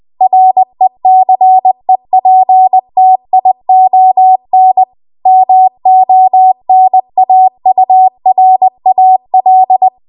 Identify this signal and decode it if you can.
Morse code, heard as a keyed tone, about 20 wpm, 750 Hz. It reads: RECEPTION MONAURAL
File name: Archive / 20 WPM / 20WPM